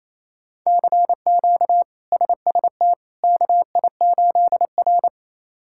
Morse code, heard as a keyed tone, about 28 wpm, 700 Hz. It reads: CQ SST KI8R
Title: K1USN SST QSO Practice - 28wpm